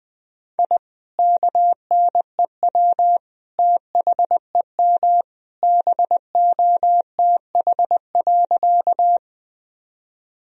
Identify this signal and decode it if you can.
Morse code, heard as a keyed tone, about 20 wpm, 700 Hz. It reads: I KNEW THEM BOTH.
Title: Sentences from Top 1000 Words - Review Part 1 - 20wpm